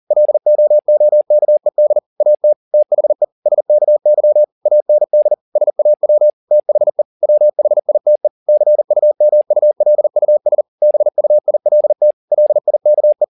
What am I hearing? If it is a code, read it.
Morse code, heard as a keyed tone, about 40 wpm, 600 Hz. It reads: LOOKED AT THE SKY AND SAW THE WHITE CUMULUS BUILT LIKE